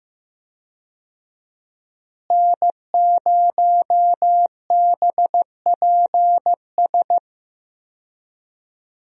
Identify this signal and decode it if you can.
Morse code, heard as a keyed tone, about 15 wpm, 700 Hz. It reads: N0BPS
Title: USA Call Signs - 5 Characters - Repeated 2x - 15wpm